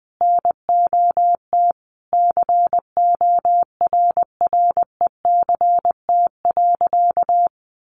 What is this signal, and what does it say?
Morse code, heard as a keyed tone, about 20 wpm, 700 Hz. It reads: NOT CORRECT.
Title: Sentences from Top 1200 Words 20wpm